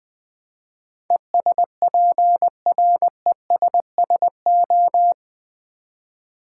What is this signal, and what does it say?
Morse code, heard as a keyed tone, about 20 wpm, 690 Hz. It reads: ESPRESSO
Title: Easy Words with Hints - 20wpm